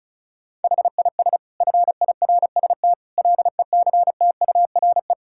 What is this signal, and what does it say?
Morse code, heard as a keyed tone, about 34 wpm, 700 Hz. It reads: HIS FIRST LECTURE
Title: Sentences from Top 1600 Words 35wpm